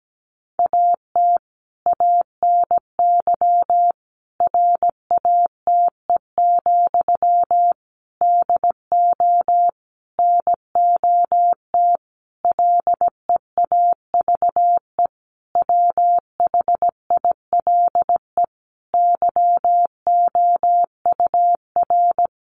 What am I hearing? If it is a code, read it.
Morse code, heard as a keyed tone, about 17 wpm, 700 Hz. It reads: AT ANY RATE, DO NOT LEAVE WHILE YOUR
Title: Sentences from Top 1500 Words 17wpm